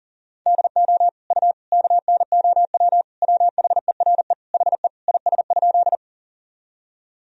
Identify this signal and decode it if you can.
Morse code, heard as a keyed tone, about 40 wpm, 700 Hz. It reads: DO U KNOW WHERE HE IS?
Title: Sentences from Top 300 Words and Abbreviations 40wpm